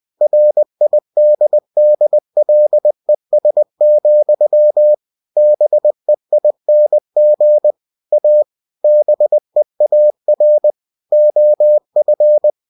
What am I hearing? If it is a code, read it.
Morse code, heard as a keyed tone, about 20 wpm, 600 Hz. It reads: RIDDLES, BEING A BEAR OF
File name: cw-books / winnie-the-pooh / winnie-20wpm-page-0045